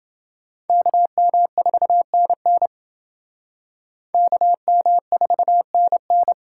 Morse code, heard as a keyed tone, about 28 wpm, 700 Hz. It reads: KM4NN KM4NN